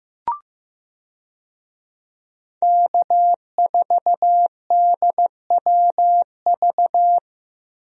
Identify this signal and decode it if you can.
Morse code, heard as a keyed tone, about 15 wpm, 700 Hz. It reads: K4DWV
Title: USA Call Signs - 5 Characters - Repeated 2x - 15wpm